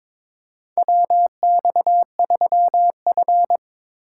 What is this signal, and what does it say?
Morse code, heard as a keyed tone, about 22 wpm, 700 Hz. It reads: WX3F
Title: USA Call Signs - 4x Character Spacing - 22wpm